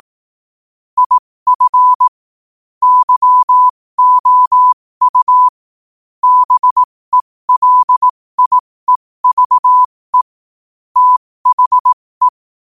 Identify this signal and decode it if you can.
Morse code, heard as a keyed tone, about 18 wpm, spaced as for 15 wpm, 1000 Hz. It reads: IF YOU BELIEVE THE
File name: qotdfiles / 15wpm